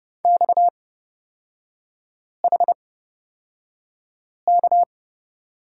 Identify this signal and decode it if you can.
Morse code, heard as a keyed tone, about 31 wpm, spaced as for 5 wpm, 700 Hz. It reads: XHK
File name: Lesson 37e - Letter x - Three Characters 30wpm